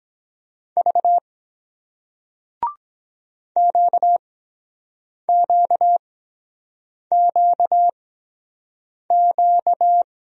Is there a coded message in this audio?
Yes: Morse code from 0.8 to 10.0 s, heard as a keyed tone, about 23 wpm, spaced as for 7 wpm, 700 Hz. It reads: V QQQQ